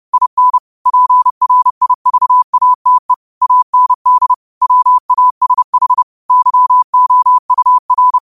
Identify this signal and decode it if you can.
Morse code, heard as a keyed tone, about 30 wpm, 1000 Hz. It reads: IN PRIVATE AND WASH YOUR